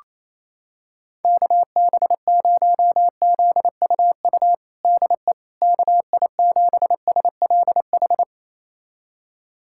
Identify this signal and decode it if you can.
Morse code, heard as a keyed tone, about 28 wpm, 700 Hz. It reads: KB0ZUU DE KI7SLH
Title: USA Callsign DE Callsign - 6 Characters - 28wpm